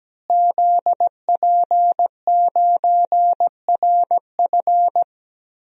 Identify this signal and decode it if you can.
Morse code, heard as a keyed tone, about 17 wpm, 700 Hz. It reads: ZP9RF
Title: DXCC-100 Call Signs - 5 Characters - Repeated 3x - 17wpm